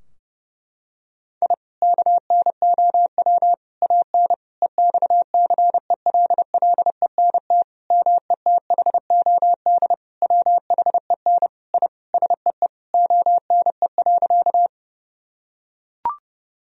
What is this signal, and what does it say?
Morse code, heard as a keyed tone, about 30 wpm, 700 Hz. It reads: I KNOW AN EXCELLENT METHOD WHEN I SEE ONE.
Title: Sentences from Top 1000 Words - Review Part 3 - 30wpm